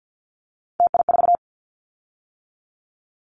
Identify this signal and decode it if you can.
Morse code, heard as a keyed tone, about 50 wpm, 700 Hz. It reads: TI4
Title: Callsign Prefixes for Top 100 DXCC - 50wpm